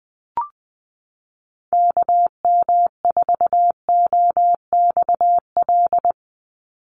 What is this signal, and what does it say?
Morse code, heard as a keyed tone, about 20 wpm, 700 Hz. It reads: KM4OXL
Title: USA Call Signs 20wpm